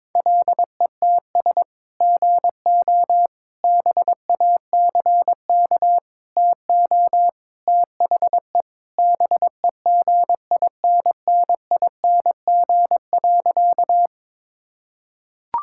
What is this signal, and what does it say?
Morse code, heard as a keyed tone, about 22 wpm, 700 Hz. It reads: LETS GO BACK TO THE BEGINNING.
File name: practice / 22wpm / Sentences with 3-Word Phrases - 22wpm